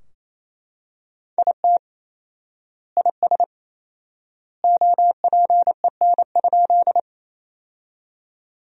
Morse code, heard as a keyed tone, about 28 wpm, 700 Hz. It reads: IT IS OPEN?
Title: 4x Word-Spacing - Sentences from Top 300 Words - 28wpm